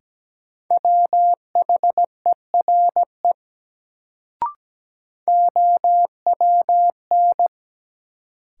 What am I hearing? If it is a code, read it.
Morse code, heard as a keyed tone, about 17 wpm, 700 Hz. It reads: WHERE OWN
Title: Sets of 2 Words - New Word Drill - 17wpm